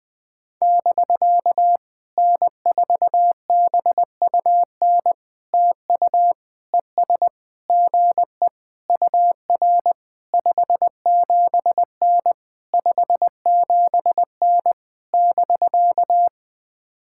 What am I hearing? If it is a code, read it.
Morse code, heard as a keyed tone, about 20 wpm, 700 Hz. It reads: <BK> N4BUN TU ES GE UR 57N 57N <BK>
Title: POTA QSO Practice - 20wpm